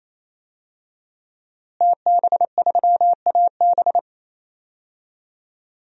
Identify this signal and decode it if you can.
Morse code, heard as a keyed tone, about 28 wpm, 700 Hz. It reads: TB3AB
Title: DXCC-100 Call Signs - 5 Characters - Repeated 3x - 28wpm